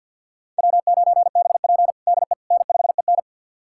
Morse code, heard as a keyed tone, about 49 wpm, 700 Hz. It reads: W9BP DE N5EN
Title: USA Callsign DE Callsign - 4 Characters - 50wpm